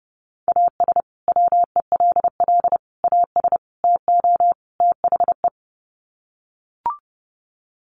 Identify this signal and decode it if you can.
Morse code, heard as a keyed tone, about 30 wpm, 700 Hz. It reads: AS WELL AS TO THE
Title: Sets of 5 Words Encore2 30wpm